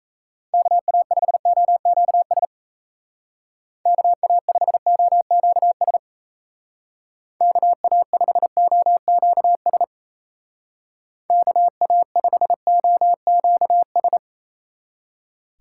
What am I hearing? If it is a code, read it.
Morse code, heard as a keyed tone, about 33 wpm, 700 Hz. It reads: KA5OQS KA5OQS KA5OQS KA5OQS
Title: Speed Racing - USA 2x3 Call Signs 28wpm